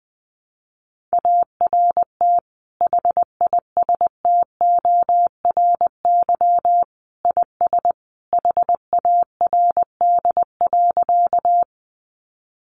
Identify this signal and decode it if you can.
Morse code, heard as a keyed tone, about 20 wpm, 700 Hz. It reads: ART HISTORY IS HARD.
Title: Sentences from Top 600 Words 20wpm